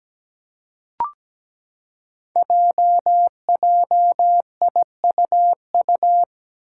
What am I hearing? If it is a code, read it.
Morse code, heard as a keyed tone, about 17 wpm, 700 Hz. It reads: JJIUU
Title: Five Challenging Letter-Number Combinations 17wpm